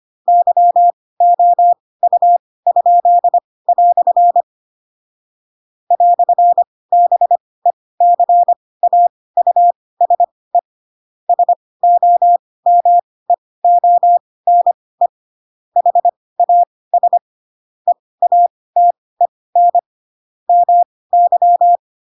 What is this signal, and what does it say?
Morse code, heard as a keyed tone, about 25 wpm, spaced as for 18 wpm, 700 Hz. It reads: YOU?" "BECAUSE SOMEONE HAS EATEN MY
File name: Pinocchio / Pinocchio ch7 12 25 ik1hge